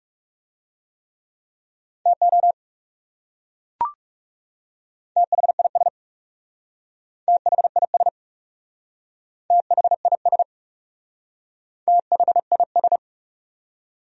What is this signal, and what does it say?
Morse code, heard as a keyed tone, about 37 wpm, 700 Hz. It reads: TO THIS THIS THIS THIS
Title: Speed-Racing Top 100 Words 30wpm